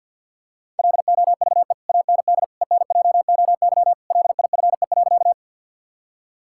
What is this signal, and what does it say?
Morse code, heard as a keyed tone, about 50 wpm, 700 Hz. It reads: LOVE AND ENJOY LIFE.
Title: Sentences from Top 1000 Words - Review Part 5 - 50wpm